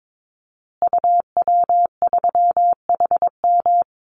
Morse code, heard as a keyed tone, about 22 wpm, 700 Hz. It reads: UW3HM
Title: Call Signs Repeated 3x - 22wpm